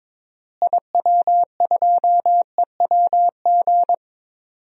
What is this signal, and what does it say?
Morse code, heard as a keyed tone, about 22 wpm, 700 Hz. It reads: IW2EWG
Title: DXCC-100 Call Signs - 6 Characters - Repeated 3x - 22wpm